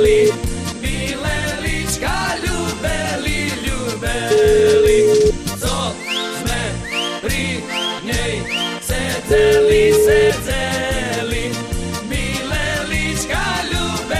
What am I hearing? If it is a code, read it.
Ringback tone (Europe, 425 Hz), with noise.